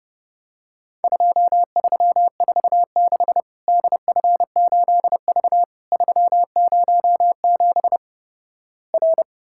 Morse code, heard as a keyed tone, about 30 wpm, 700 Hz. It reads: 2346 DF8V 307 R
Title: Worked All Europe DX Contest - QTC Exchange - 30wpm